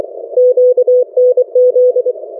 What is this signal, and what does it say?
Morse code, heard as a keyed tone, about 24 wpm, 500 Hz. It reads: QNZ
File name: RRIE-Net-2mins